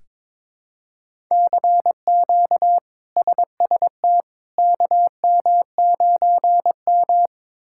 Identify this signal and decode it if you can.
Morse code, heard as a keyed tone, about 22 wpm, 700 Hz. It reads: CQ SST KM9M
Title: K1USN SST QSO Practice - 22wpm